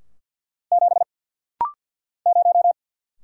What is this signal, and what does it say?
Morse code, heard as a keyed tone, about 49 wpm, 700 Hz. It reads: 7 0
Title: Single Letter-Number - Rapid-Fire 50wpm